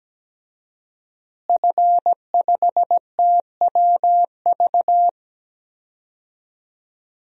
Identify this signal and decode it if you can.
Morse code, heard as a keyed tone, about 17 wpm, 700 Hz. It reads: F5TWV
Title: DXCC-100 Call Signs - 5 Characters - Repeated 3x - 17wpm